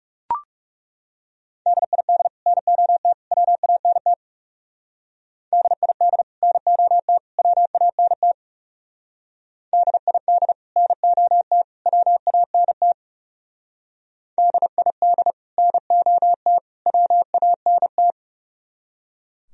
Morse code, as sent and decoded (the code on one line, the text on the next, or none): -.. .. -.. / -. --- - / .-- .- -. - / -.. .. -.. / -. --- - / .-- .- -. - / -.. .. -.. / -. --- - / .-- .- -. - / -.. .. -.. / -. --- - / .-- .- -. -
DID NOT WANT DID NOT WANT DID NOT WANT DID NOT WANT